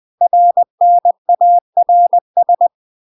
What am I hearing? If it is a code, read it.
Morse code, heard as a keyed tone, about 20 wpm, 700 Hz. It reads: RNARS
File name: 20WPM